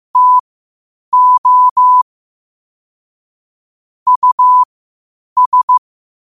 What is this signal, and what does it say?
Morse code, heard as a keyed tone, about 15 wpm, spaced as for 8 wpm, 1000 Hz. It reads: TO US
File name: qotdfiles / 8wpm